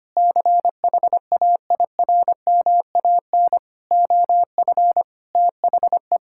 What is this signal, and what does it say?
Morse code, heard as a keyed tone, about 25 wpm, 700 Hz. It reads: CHAIRMAN OF THE
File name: practice / 25wpm / Sentences with 3-Word Phrases - 25wpm